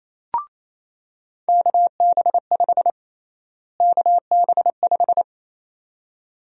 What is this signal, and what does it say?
Morse code, heard as a keyed tone, about 28 wpm, 700 Hz. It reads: KB5 KB5
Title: USA Call Sign Prefixes - Repeated 2x - 28wpm